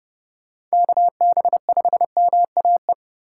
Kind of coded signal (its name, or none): Morse code